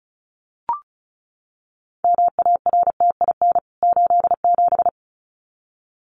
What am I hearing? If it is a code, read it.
Morse code, heard as a keyed tone, about 34 wpm, 700 Hz. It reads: MARTIN 87
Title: CWT Exchange 35wpm